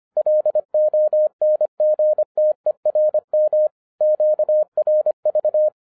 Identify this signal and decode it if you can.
Morse code, heard as a keyed tone, about 25 wpm, 610 Hz. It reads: LONGTERM QRV